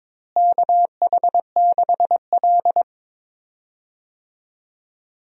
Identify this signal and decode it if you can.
Morse code, heard as a keyed tone, about 22 wpm, 700 Hz. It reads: KH6L
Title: DXCC-100 Call Signs - 4 Characters - Repeated 3x - 22wpm